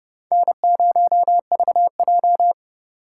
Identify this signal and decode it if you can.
Morse code, heard as a keyed tone, about 30 wpm, 700 Hz. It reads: N0VJ
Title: USA Call Signs - 4 Characters - 4x Character Spacing - 30wpm